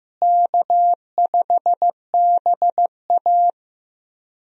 Morse code, heard as a keyed tone, about 15 wpm, 700 Hz. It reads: K5BA
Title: USA Call Signs - 4 Characters - Repeated 2x - 15wpm